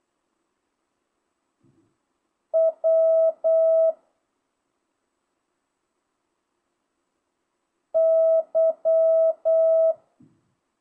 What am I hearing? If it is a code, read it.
Morse code, heard as a keyed tone, about 8 wpm, 650 Hz. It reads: W Y
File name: jh8cbh / 2025cw